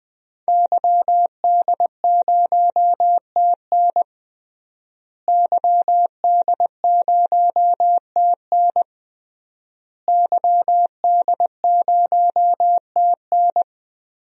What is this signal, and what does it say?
Morse code, heard as a keyed tone, about 20 wpm, 700 Hz. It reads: YD0TN YD0TN YD0TN
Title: DXCC-100 Call Signs - 5 Characters - Repeated 3x - 20wpm